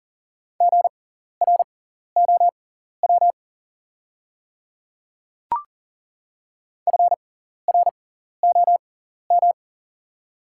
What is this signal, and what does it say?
Morse code, heard as a keyed tone, about 40 wpm, spaced as for 14 wpm, 690 Hz. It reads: GROW FROM
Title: 6x Character Spacing - Top 300 Words - 40wpm